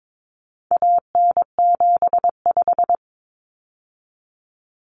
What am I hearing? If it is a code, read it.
Morse code, heard as a keyed tone, about 22 wpm, 700 Hz. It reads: AN75
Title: Common Cut Numbers - 4-Digit 22wpm